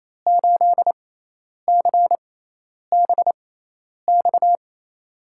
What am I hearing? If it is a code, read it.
Morse code, heard as a keyed tone, about 28 wpm, spaced as for 10 wpm, 700 Hz. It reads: 8CBX
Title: USA Call Signs - 5 Characters - 6x Character Spacing - 28wpm